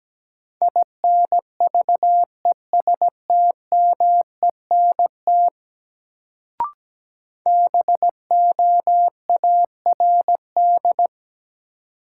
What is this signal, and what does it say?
Morse code, heard as a keyed tone, about 17 wpm, 700 Hz. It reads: INVESTMENT BOARD
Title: Sentences from Top 700 Words - New Word Drill - 17wpm